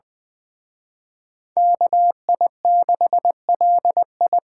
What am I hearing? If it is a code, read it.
Morse code, heard as a keyed tone, about 24 wpm, 700 Hz. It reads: KI6LI
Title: Speed Racing - USA Call Signs 20wpm